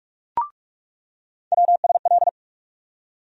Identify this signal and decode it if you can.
Morse code, heard as a keyed tone, about 46 wpm, 700 Hz. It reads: WSL
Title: USA Call Sign Postfixes - 3 Letters 45wpm